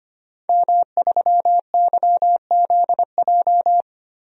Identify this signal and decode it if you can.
Morse code, heard as a keyed tone, about 25 wpm, 700 Hz. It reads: M3YZJ
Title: DXCC-100 Call Signs - 5 Characters - 25wpm